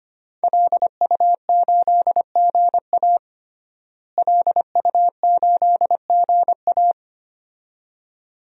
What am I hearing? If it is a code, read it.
Morse code, heard as a keyed tone, about 25 wpm, 700 Hz. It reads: LU8GA LU8GA
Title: DXCC-100 Call Signs - 5 Characters - Repeated 2x - 25wpm